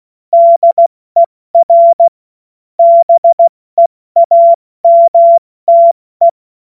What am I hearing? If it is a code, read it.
Morse code, heard as a keyed tone, about 16 wpm, 680 Hz. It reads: DER BEAMTE
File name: Morsekurs / Witze / Witz 19. April 2022 70 F80